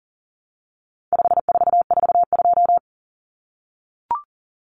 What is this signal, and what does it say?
Morse code, heard as a keyed tone, about 40 wpm, 700 Hz. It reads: 5442
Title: Numbers - 4-Digit 40wpm